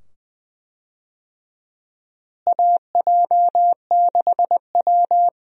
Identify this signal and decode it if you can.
Morse code, heard as a keyed tone, about 20 wpm, 700 Hz. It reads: AJ6W
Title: USA Call Signs - 4 Characters - Repeated 2x - 20wpm